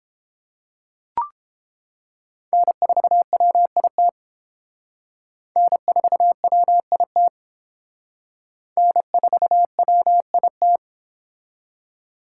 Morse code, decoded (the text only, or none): N4WIT N4WIT N4WIT